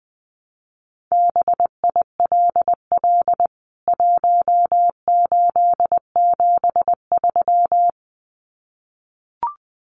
Morse code, heard as a keyed tone, about 19 wpm, 700 Hz. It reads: BILL 1873
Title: CWT Exchange 20wpm